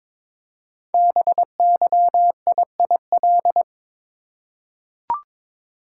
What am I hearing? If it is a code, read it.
Morse code, heard as a keyed tone, about 22 wpm, 700 Hz. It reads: BYIIL
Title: Code Groups - 5 Letters - 22wpm